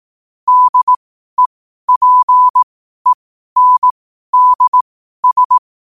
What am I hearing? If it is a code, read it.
Morse code, heard as a keyed tone, about 18 wpm, spaced as for 12 wpm, 1000 Hz. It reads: DEPENDS